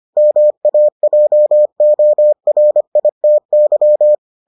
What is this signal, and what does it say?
Morse code, heard as a keyed tone, about 25 wpm, 600 Hz. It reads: MAJORITY